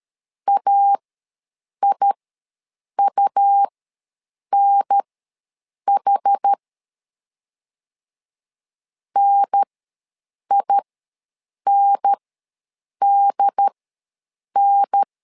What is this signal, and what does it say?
Morse code, heard as a keyed tone, about 13 wpm, spaced as for 7 wpm, 800 Hz. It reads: AIUNH NINDN
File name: LEZIONI / 2 UHDN / 2 UHDN ws 12wpm 1708